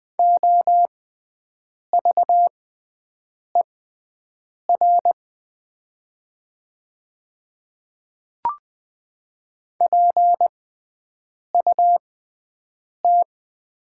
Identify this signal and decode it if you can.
Morse code, heard as a keyed tone, about 20 wpm, spaced as for 7 wpm, 700 Hz. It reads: OVER PUT
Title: 6x Character Spacing - Top 200 Words - 20wpm